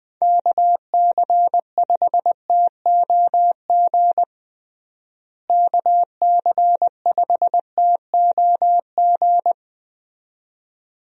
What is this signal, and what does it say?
Morse code, heard as a keyed tone, about 20 wpm, 700 Hz. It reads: KC5TOG KC5TOG
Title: USA Call Signs - 6 Characters - Repeated 2x - 20wpm